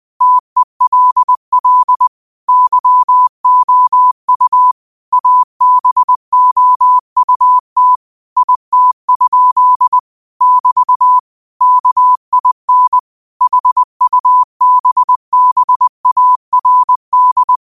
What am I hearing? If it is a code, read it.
Morse code, heard as a keyed tone, about 20 wpm, 1000 Hz. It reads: TELL YOU ABOUT IT? <BT> KIN HUBBARD
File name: qotdfiles / 20wpm